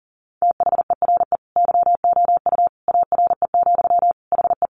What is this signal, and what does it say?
Morse code, heard as a keyed tone, about 40 wpm, 700 Hz. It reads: THERE YOU ARE, HE
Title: Sentences from Top 100 Words 40wpm